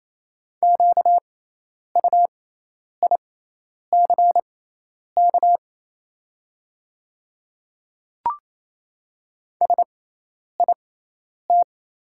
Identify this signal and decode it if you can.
Morse code, heard as a keyed tone, about 28 wpm, spaced as for 10 wpm, 690 Hz. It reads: QUICK SIT
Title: 6x Character Spacing - Top 500 Words - 28wpm